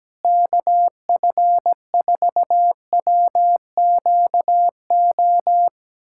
Morse code, heard as a keyed tone, about 17 wpm, 700 Hz. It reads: KF4WQO
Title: USA Callsign DE Callsign - 6 Characters - 17wpm